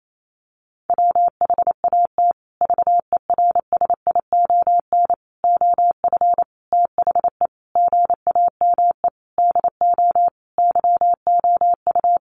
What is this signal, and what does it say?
Morse code, heard as a keyed tone, about 28 wpm, 700 Hz. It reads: WHAT VERSION OF THE GAME DO YOU
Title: Sentences from Top 1400 Words 28wpm